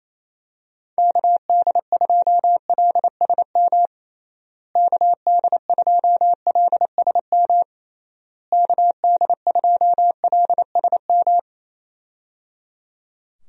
Morse code, heard as a keyed tone, about 27 wpm, 700 Hz. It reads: KD2LSM KD2LSM KD2LSM
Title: USA Call Signs - 6 Characters - Repeated 3x - 28wpm